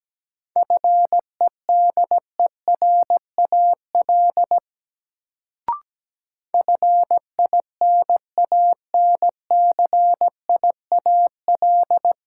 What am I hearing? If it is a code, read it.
Morse code, heard as a keyed tone, about 17 wpm, 700 Hz. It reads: FEDERAL FINANCIAL
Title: Sentences from Top 1000 Words - Review Part 5 - New Word Drill - 17wpm